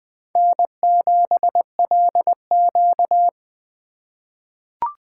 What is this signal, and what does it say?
Morse code, heard as a keyed tone, about 20 wpm, 700 Hz. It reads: N7LQ
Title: USA Call Signs - 4 Characters - 4x Character Spacing - 20wpm